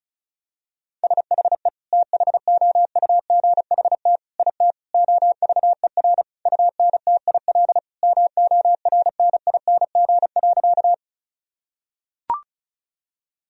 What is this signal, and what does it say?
Morse code, heard as a keyed tone, about 36 wpm, 700 Hz. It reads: SHE THOUGHT IT OVER UNTIL MORNING.
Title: Sentences from Top 1000 Words - Review Part 1 - 35wpm